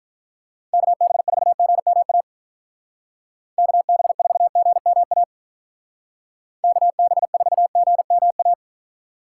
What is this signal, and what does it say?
Morse code, heard as a keyed tone, about 43 wpm, 700 Hz. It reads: KB4GMA KB4GMA KB4GMA
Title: Speed Racing - USA 2x3 Call Signs 35wpm